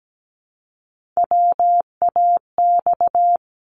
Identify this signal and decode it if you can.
Morse code, heard as a keyed tone, about 17 wpm, 700 Hz. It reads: WAX
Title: Multi-letter Phonemes Repeated 3x 17wpm